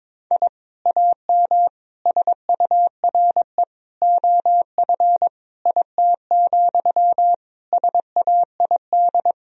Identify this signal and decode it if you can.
Morse code, heard as a keyed tone, about 22 wpm, 700 Hz. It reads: I AM SURE OF IT, SAID
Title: Sentences from Top 1000 Words - Review Part 4 - 22wpm